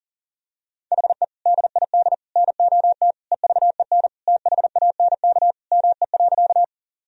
Morse code, heard as a keyed tone, about 40 wpm, 700 Hz. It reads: HE DID NOT EVEN THANK ME.
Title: Sentences from Top 1000 Words - Review Part 1 - 40wpm